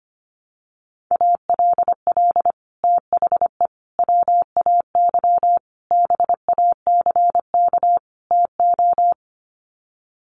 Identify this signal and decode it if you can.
Morse code, heard as a keyed tone, about 25 wpm, 700 Hz. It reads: ALL THE WAY BACK TO
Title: Sets of 5 Words Encore 25wpm